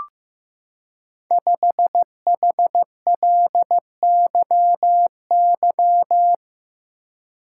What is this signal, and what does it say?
Morse code, heard as a keyed tone, about 15 wpm, 700 Hz. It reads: 5HLYY
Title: Code Groups - 5 Letter:Numbers - 15wpm